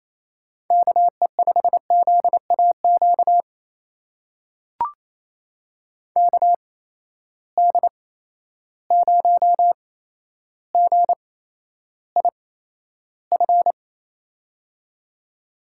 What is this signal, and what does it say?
Morse code, heard as a keyed tone, about 28 wpm, spaced as for 12 wpm, 700 Hz. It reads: KE5ZAQ KD0GIF
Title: USA Call Signs - 6 Characters - 8x Character Spacing - 28wpm